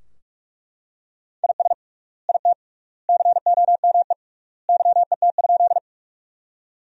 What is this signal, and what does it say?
Morse code, heard as a keyed tone, about 41 wpm, 700 Hz. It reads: IS IT COME YET?
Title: 3x Word-Spacing - Sentences from Top 500 Words - 45wpm